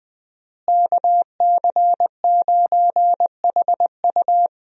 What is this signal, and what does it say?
Morse code, heard as a keyed tone, about 20 wpm, 700 Hz. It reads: KC9HU
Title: USA Call Signs - 5 Characters - Repeated 3x - 20wpm